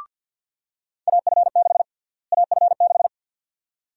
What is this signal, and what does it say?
Morse code, heard as a keyed tone, about 46 wpm, 700 Hz. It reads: AF6 AF6
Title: USA Call Sign Prefixes - Repeated 2x - 50wpm